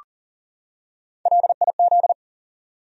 Morse code, heard as a keyed tone, about 40 wpm, 700 Hz. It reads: LIZ
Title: USA Common Names - 40wpm